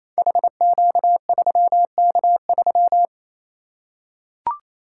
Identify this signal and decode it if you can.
Morse code, heard as a keyed tone, about 28 wpm, 700 Hz. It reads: HQ3K3
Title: Code Groups - 5 Letter:Numbers - 28wpm